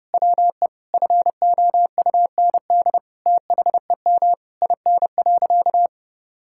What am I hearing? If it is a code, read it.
Morse code, heard as a keyed tone, about 30 wpm, 700 Hz. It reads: WE FOUND THEM IN.
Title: Sentences from Top 1000 Words - Review Part 6 - 30wpm